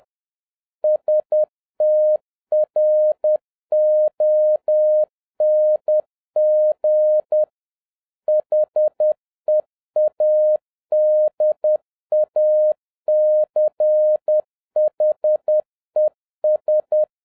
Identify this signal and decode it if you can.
Morse code, heard as a keyed tone, about 10 wpm, 610 Hz. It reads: STRONG HEADACHES
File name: 2011-07-mp3 / n56 BACK FROM A BIG ACCIDENT 10wpm